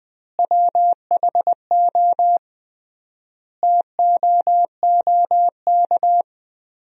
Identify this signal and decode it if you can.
Morse code, heard as a keyed tone, about 21 wpm, 700 Hz. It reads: WHO TOOK
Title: 3x Word-Spacing - Sentences from Top 300 Words - 20wpm